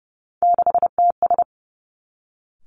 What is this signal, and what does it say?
Morse code, heard as a keyed tone, about 30 wpm, 700 Hz. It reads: 6TS
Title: Three Challenging Letter-Number Combinations 30wpm